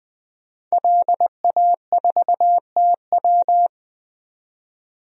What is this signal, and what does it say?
Morse code, heard as a keyed tone, about 20 wpm, 700 Hz. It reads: LA4TW
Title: DXCC-100 Call Signs - 5 Characters - 20wpm